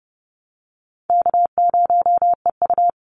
Morse code, heard as a keyed tone, about 30 wpm, 700 Hz. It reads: K0EU
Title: Call Signs Repeated 2x - 30wpm